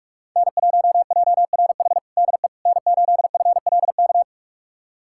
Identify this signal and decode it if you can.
Morse code, heard as a keyed tone, about 46 wpm, 700 Hz. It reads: N1JRH DE N8FLK
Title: USA Callsign DE Callsign - 5 Characters - 45wpm